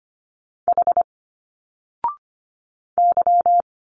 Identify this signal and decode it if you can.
Morse code, heard as a keyed tone, about 25 wpm, 700 Hz. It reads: H Y
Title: Single Letters 25wpm